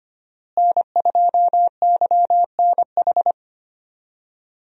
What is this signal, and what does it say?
Morse code, heard as a keyed tone, about 25 wpm, 700 Hz. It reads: N2YNH
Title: USA Call Signs - 5 Characters - Repeated 3x - 25wpm